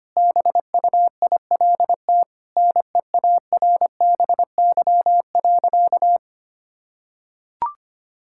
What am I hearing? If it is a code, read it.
Morse code, heard as a keyed tone, about 25 wpm, 700 Hz. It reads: BUILT NEARBY.
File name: practice / 25wpm / Sentences with 2-Word Phrases - 25wpm